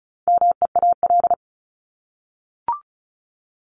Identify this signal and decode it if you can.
Morse code, heard as a keyed tone, about 34 wpm, 700 Hz. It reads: MEAL
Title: Top 1751-2000 Words 35wpm